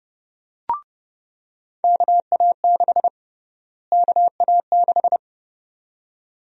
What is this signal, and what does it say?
Morse code, heard as a keyed tone, about 30 wpm, 700 Hz. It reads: KA6 KA6
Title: USA Call Sign Prefixes - Repeated 2x - 30wpm